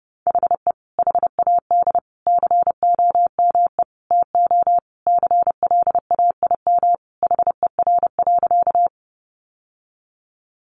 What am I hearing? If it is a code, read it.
Morse code, heard as a keyed tone, about 30 wpm, 700 Hz. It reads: HE HAD COME TO CLAIM HER.
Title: Sentences from Top 1500 Words 30wpm